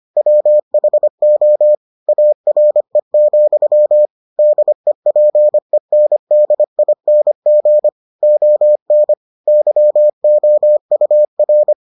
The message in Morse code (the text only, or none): WHO ARE, DEPENDING ON YOUR